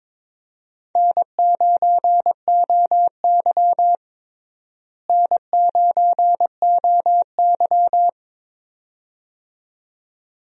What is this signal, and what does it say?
Morse code, heard as a keyed tone, about 21 wpm, 700 Hz. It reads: N9OY N9OY
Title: USA Call Signs - 4 Characters - Repeated 2x - 22wpm